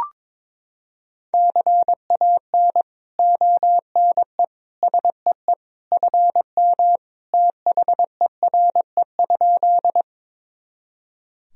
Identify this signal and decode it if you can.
Morse code, heard as a keyed tone, about 22 wpm, 700 Hz. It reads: CAN ONE SEE FM THERE?
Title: Sentences from Top 100 Words and Abbreviations 22wpm